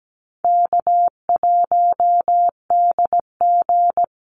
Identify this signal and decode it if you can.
Morse code, heard as a keyed tone, about 17 wpm, 700 Hz. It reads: K1DG
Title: Call Signs Repeated 2x - 17wpm